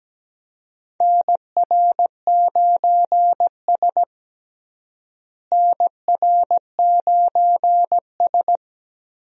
Morse code, heard as a keyed tone, about 17 wpm, 700 Hz. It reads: NR9S NR9S
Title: USA Call Signs - 4 Characters - Repeated 2x - 17wpm